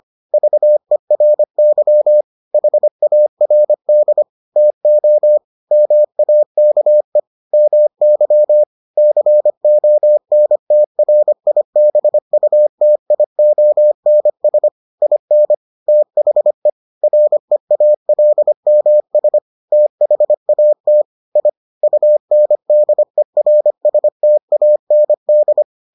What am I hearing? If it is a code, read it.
Morse code, heard as a keyed tone, about 25 wpm, 600 Hz. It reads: VERY HARD TO MAKE MY CONTRIBUTIONS IN THE REALMS THAT I UNDERSTAND